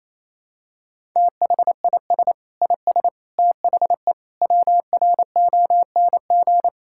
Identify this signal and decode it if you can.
Morse code, heard as a keyed tone, about 28 wpm, 700 Hz. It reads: THIS IS THE WRONG
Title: Sentences from Top 1000 Words - Review Part 6 - 28wpm